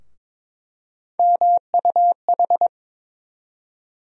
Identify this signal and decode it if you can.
Morse code, heard as a keyed tone, about 22 wpm, 700 Hz. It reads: MUH
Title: USA Call Sign Postfixes - 3 Letters 22wpm